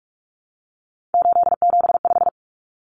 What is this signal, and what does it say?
Morse code, heard as a keyed tone, about 44 wpm, 700 Hz. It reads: 875
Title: Numbers - 3-Digit 45wpm